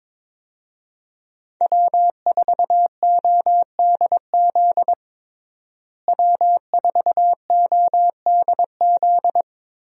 Morse code, heard as a keyed tone, about 22 wpm, 700 Hz. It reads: W4ODZ W4ODZ
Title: USA Call Signs - 5 Characters - Repeated 3x - 22wpm